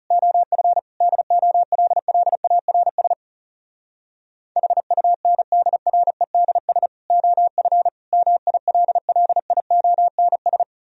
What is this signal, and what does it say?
Morse code, heard as a keyed tone, about 40 wpm, 700 Hz. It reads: OF DOLLARS HUNDREDS OF MILLIONS